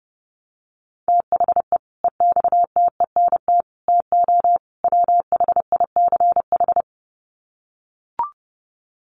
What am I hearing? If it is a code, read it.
Morse code, heard as a keyed tone, about 30 wpm, 700 Hz. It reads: THE EXTENT TO WHICH
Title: Sets of 4 Words 30wpm